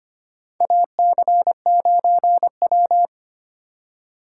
Morse code, heard as a keyed tone, about 25 wpm, 700 Hz. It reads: AC9W
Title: USA Call Signs - 4 Characters - 2x Character Spacing - 25wpm